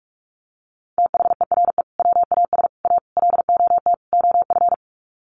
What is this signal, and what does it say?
Morse code, heard as a keyed tone, about 44 wpm, 700 Hz. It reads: THERE WAS A LOT OF
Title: Sets of 5 Words 45wpm